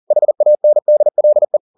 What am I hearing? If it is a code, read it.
Morse code, heard as a keyed tone, about 40 wpm, 600 Hz. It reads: HANDLE